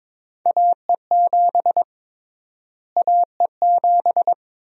Morse code, heard as a keyed tone, about 22 wpm, 700 Hz. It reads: AE7 AE7
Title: USA Call Sign Prefixes - Repeated 2x - 22wpm